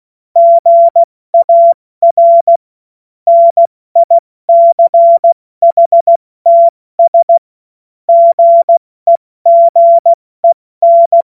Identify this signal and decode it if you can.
Morse code, heard as a keyed tone, about 16 wpm, 680 Hz. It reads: GAR NICHTS GEGEN
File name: Morsekurs / Witze / Witz 01. Februar 2022 70 F80